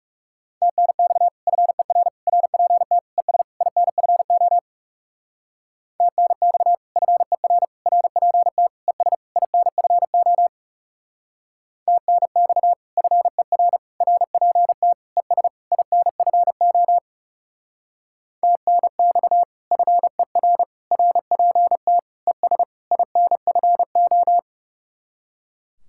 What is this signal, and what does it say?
Morse code, heard as a keyed tone, about 38 wpm, 700 Hz. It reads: TNX FER RPT ES INFO TNX FER RPT ES INFO TNX FER RPT ES INFO TNX FER RPT ES INFO